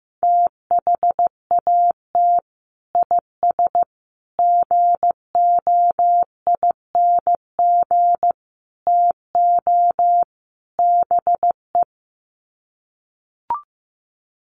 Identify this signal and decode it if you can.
Morse code, heard as a keyed tone, about 15 wpm, 700 Hz. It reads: THAT IS GOING TO BE
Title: Sets of 5 Words Encore2 15wpm